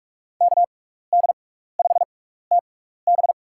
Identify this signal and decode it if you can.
Morse code, heard as a keyed tone, about 42 wpm, spaced as for 15 wpm, 700 Hz. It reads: KD5TB
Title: USA Call Signs - 5 Characters - 6x Character Spacing - 45wpm